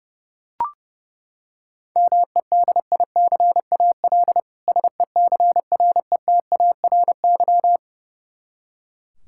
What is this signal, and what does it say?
Morse code, heard as a keyed tone, about 30 wpm, 700 Hz. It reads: MEDICAL SECRETARY
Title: Practice QSO - Exchange 3 - More Info - 30wpm